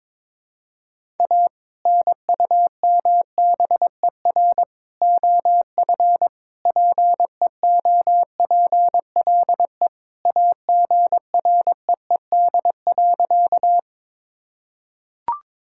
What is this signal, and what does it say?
Morse code, heard as a keyed tone, about 22 wpm, 700 Hz. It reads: A NUMBER OF PEOPLE AGREED.
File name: practice / 22wpm / Sentences with 3-Word Phrases - 22wpm